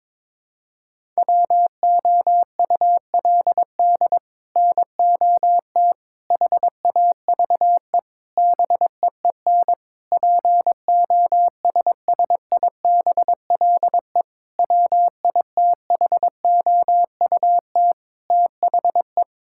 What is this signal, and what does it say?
Morse code, heard as a keyed tone, about 22 wpm, 700 Hz. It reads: WOULD NOT HAVE BEEN POSSIBLE WITHOUT THE
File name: practice / 22wpm / Sets of 7 Words - 22wpm